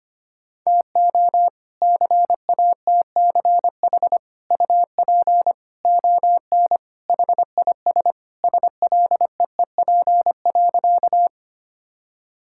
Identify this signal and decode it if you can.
Morse code, heard as a keyed tone, about 25 wpm, 700 Hz. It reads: TO CATCH UP ON HIS SLEEP.
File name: practice / 25wpm / All-Age Appropriate Jokes - 25wpm